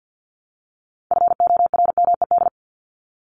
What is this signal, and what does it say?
Morse code, heard as a keyed tone, about 50 wpm, 700 Hz. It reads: FORMED
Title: ED Suffix - Top 100 Words with Base Words 50wpm